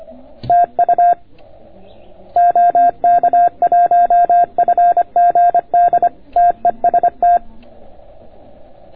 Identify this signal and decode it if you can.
Morse code, heard as a keyed tone, about 24 wpm, 690 Hz. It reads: TU OK1FGD TEST